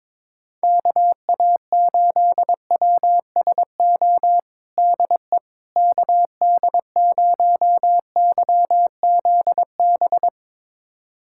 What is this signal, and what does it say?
Morse code, heard as a keyed tone, about 22 wpm, 700 Hz. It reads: KA8WSO DE KD0YZB
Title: USA Callsign DE Callsign - 6 Characters - 22wpm